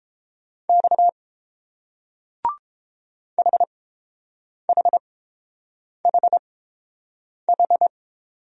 Morse code, heard as a keyed tone, about 30 wpm, spaced as for 7 wpm, 690 Hz. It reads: X HHHH